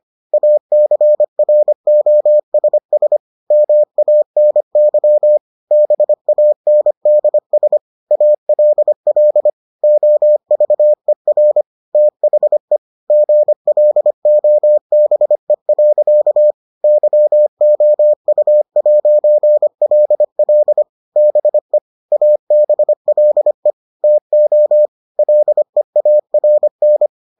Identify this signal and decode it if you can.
Morse code, heard as a keyed tone, about 25 wpm, 600 Hz. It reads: ACROSS MANY BANDS ALL OVER THE GLOBE. YOU'LL BE ABLE TO LEARN